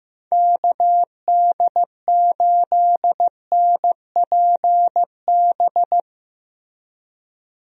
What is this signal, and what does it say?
Morse code, heard as a keyed tone, about 15 wpm, 700 Hz. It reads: KD8NPB
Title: USA Call Signs - 6 Characters - Repeated 3x - 15wpm